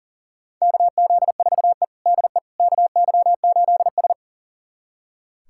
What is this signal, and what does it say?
Morse code, heard as a keyed tone, about 40 wpm, 700 Hz. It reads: KZ4E DE KY8S
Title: USA Callsign DE Callsign - 4 Characters - 40wpm